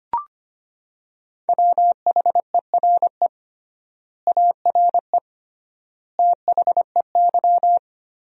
Morse code, heard as a keyed tone, about 25 wpm, 700 Hz. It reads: WHERE ARE THEY
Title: 3x Word-Spacing - Sentences from Top 200 Words - 25wpm